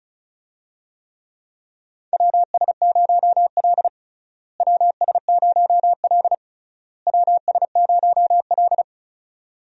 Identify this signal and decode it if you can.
Morse code, heard as a keyed tone, about 36 wpm, 700 Hz. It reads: WS0L WS0L WS0L
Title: USA Call Signs - 4 Characters - Repeated 3x - 35wpm